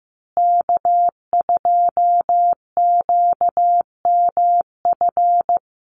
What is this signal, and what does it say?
Morse code, heard as a keyed tone, about 15 wpm, 700 Hz. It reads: K2QMF
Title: Call Signs Repeated 2x - 15wpm